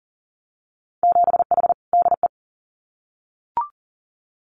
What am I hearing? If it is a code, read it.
Morse code, heard as a keyed tone, about 38 wpm, 700 Hz. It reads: 7H DE
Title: Winter Field Day Exchanges Encore 40wpm